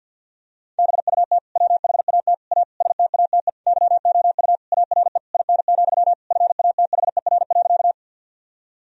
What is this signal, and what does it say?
Morse code, heard as a keyed tone, about 52 wpm, 700 Hz. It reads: BUT WHAT A STATE YOU ARE IN, FATHER.
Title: Sentences from Top 1000 Words - Review Part 2 - 50wpm